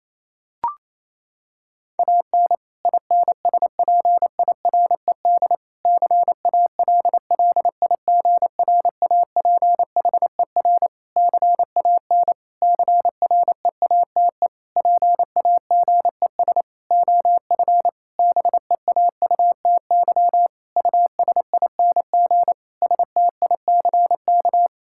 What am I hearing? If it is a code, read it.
Morse code, heard as a keyed tone, about 28 wpm, 700 Hz. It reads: AN INSPIRED CALLIGRAPHER CAN CREATE PAGES OF BEAUTY USING STICK
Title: Pangram Challenge v2 - 28wpm